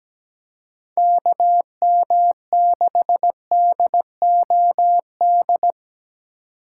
Morse code, heard as a keyed tone, about 17 wpm, 700 Hz. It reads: KM6DOD
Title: USA Call Signs - 2x Character Spacing - 17wpm